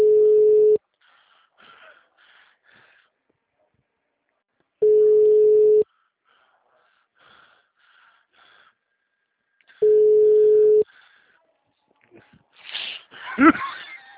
Ringback tone (Europe, 425 Hz).